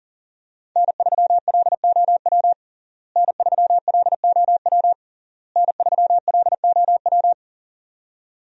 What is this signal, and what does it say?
Morse code, heard as a keyed tone, about 40 wpm, 700 Hz. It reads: N3LOW N3LOW N3LOW
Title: USA Call Signs - 5 Characters - Repeated 3x - 40wpm